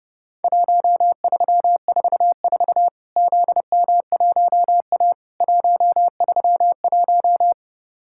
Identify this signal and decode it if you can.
Morse code, heard as a keyed tone, about 30 wpm, 700 Hz. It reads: 1344 ZM1A 131
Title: Worked All Europe DX Contest - QTC Exchange - 30wpm